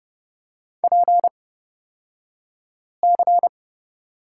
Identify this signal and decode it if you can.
Morse code, heard as a keyed tone, about 30 wpm, spaced as for 5 wpm, 700 Hz. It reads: PC
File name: Lesson 38d - Letter q - Two Characters 30wpm